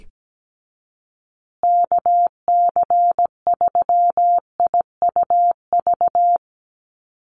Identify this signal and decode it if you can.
Morse code, heard as a keyed tone, about 17 wpm, 700 Hz. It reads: KC3IUV
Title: USA Call Signs Repeated 2x 17wpm